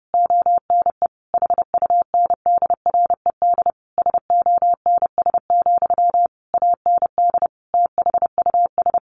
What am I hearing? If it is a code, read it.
Morse code, heard as a keyed tone, about 30 wpm, 700 Hz. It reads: ONE HUNDRED SONS, AND THUS
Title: A Sound Of Thunder 30wpm